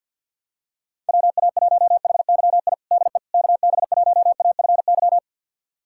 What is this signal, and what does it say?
Morse code, heard as a keyed tone, about 53 wpm, 700 Hz. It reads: WA1HYI DE KB1AFY
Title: USA Callsign DE Callsign - 6 Characters - 50wpm